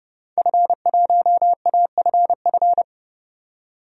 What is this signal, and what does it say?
Morse code, heard as a keyed tone, about 30 wpm, 700 Hz. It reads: F1AFF
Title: DXCC-100 Call Signs - 5 Characters - 30wpm